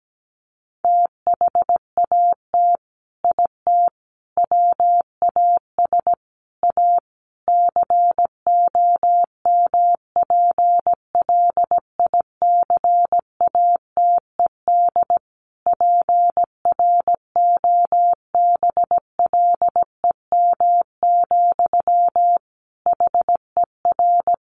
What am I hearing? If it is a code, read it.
Morse code, heard as a keyed tone, about 17 wpm, 700 Hz. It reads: THAT IT WAS A COMPLICATED PROBLEM, HER
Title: Sentences from Top 1200 Words 17wpm